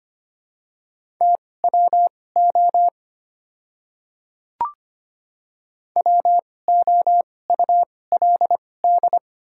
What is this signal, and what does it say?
Morse code, heard as a keyed tone, about 25 wpm, spaced as for 18 wpm, 700 Hz. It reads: TWO WOULD
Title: 2x Character Spacing - Top 100 Words - 25wpm